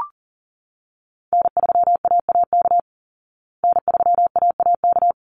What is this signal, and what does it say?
Morse code, heard as a keyed tone, about 37 wpm, 700 Hz. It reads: N3AAK N3AAK
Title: Call Signs Repeated 3x - 40wpm